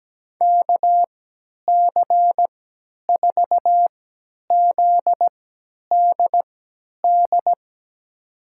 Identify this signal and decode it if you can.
Morse code, heard as a keyed tone, about 17 wpm, spaced as for 10 wpm, 700 Hz. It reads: KC4ZDD
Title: USA Call Signs - 3x Character Spacing - 17wpm